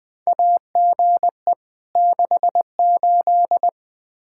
Morse code, heard as a keyed tone, about 21 wpm, 700 Hz. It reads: AGE 68
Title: Practice QSO - Exchange 3 - More Info - 20wpm